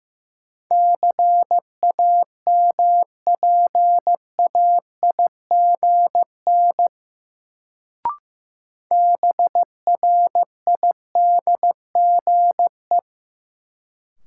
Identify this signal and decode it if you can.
Morse code, heard as a keyed tone, about 15 wpm, 700 Hz. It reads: CAMPAIGN BRIDGE
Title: Sentences from Top 1200 Words - New Word Drill - 15wpm